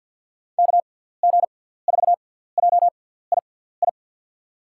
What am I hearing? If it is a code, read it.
Morse code, heard as a keyed tone, about 49 wpm, spaced as for 17 wpm, 700 Hz. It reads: KG4JII
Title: USA Call Signs - 6 Characters - 6x Character Spacing - 50wpm